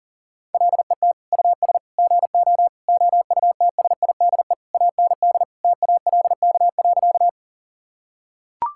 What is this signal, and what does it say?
Morse code, heard as a keyed tone, about 40 wpm, 700 Hz. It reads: LET US GO OUTSIDE AND TALK.
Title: Sentences from Top 1000 Words - Review Part 6 - 40wpm